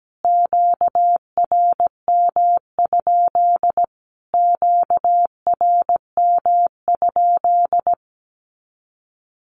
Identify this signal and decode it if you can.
Morse code, heard as a keyed tone, about 17 wpm, 700 Hz. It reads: QRM? QRM?
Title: Q-Codes Repeated 3x 17wpm